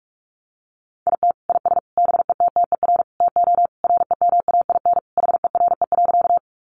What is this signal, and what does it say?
Morse code, heard as a keyed tone, about 44 wpm, 700 Hz. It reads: IT IS BETTER TO REMAIN HERE.
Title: Sentences from Top 1200 Words 45wpm